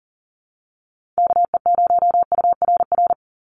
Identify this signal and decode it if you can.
Morse code, heard as a keyed tone, about 38 wpm, 700 Hz. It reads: KE0URR
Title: USA Call Signs 40wpm